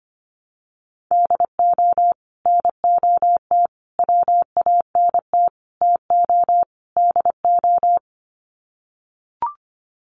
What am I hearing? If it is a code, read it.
Morse code, heard as a keyed tone, about 25 wpm, 700 Hz. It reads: DO NOT WANT TO DO
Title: Sets of 5 Words 25wpm